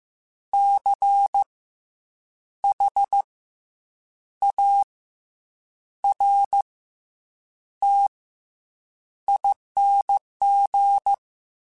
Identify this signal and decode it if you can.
Morse code, heard as a keyed tone, about 15 wpm, 800 Hz. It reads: C H A R T ING
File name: Special INGs / Special INGs-mp3s / SlowToFaster ing @13